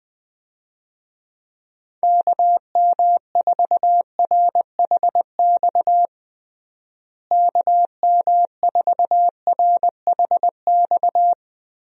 Morse code, heard as a keyed tone, about 20 wpm, 700 Hz. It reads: KM4RHX KM4RHX
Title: USA Call Signs - 6 Characters - Repeated 3x - 20wpm